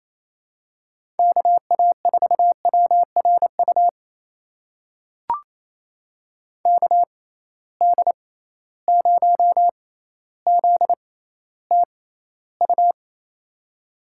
Morse code, heard as a keyed tone, about 28 wpm, spaced as for 14 wpm, 700 Hz. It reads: KA4WRU KD0ZTU
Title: USA Call Signs - 6 Characters - 6x Character Spacing - 28wpm